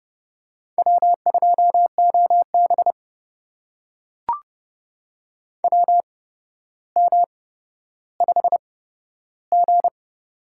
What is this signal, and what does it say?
Morse code, heard as a keyed tone, about 29 wpm, spaced as for 13 wpm, 700 Hz. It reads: W2OB WM5G
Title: USA Call Signs - 4 Characters - 8x Character Spacing - 30wpm